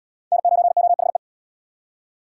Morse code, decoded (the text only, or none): N9OHE